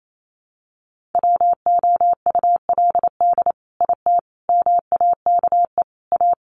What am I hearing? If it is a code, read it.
Morse code, heard as a keyed tone, about 28 wpm, 700 Hz. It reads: WOULD IT MAKE A
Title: Sentences from Top 800 Words 28wpm